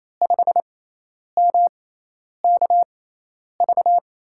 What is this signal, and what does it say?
Morse code, heard as a keyed tone, about 28 wpm, spaced as for 10 wpm, 700 Hz. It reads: 5MKV
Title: USA Call Signs - 6x Character Spacing - 28wpm